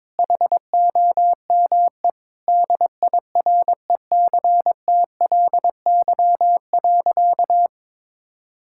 Morse code, heard as a keyed tone, about 22 wpm, 700 Hz. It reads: HOME DIRECTLY.
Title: Sentences from Top 1000 Words - Review Part 5 - 22wpm